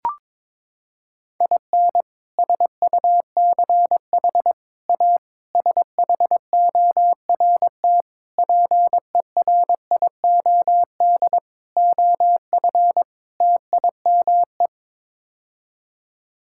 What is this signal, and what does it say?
Morse code, heard as a keyed tone, about 22 wpm, 700 Hz. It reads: IN SUCH A SHORT PERIOD OF TIME
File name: practice / 22wpm / Sets of 7 Words - 22wpm